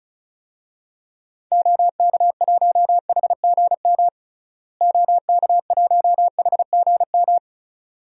Morse code, heard as a keyed tone, about 36 wpm, 700 Hz. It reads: OK1HGM OK1HGM
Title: DXCC-100 Call Signs - 6 Characters - Repeated 2x - 35wpm